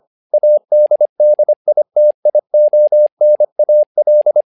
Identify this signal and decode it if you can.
Morse code, heard as a keyed tone, about 25 wpm, 600 Hz. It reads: ADDITIONAL
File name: learning center / tower trouble / tower trouble 25wpm